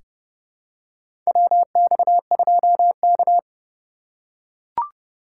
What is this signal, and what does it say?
Morse code, heard as a keyed tone, about 30 wpm, 700 Hz. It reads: WX2K
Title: USA Call Signs - 3x Character Spacing - 30wpm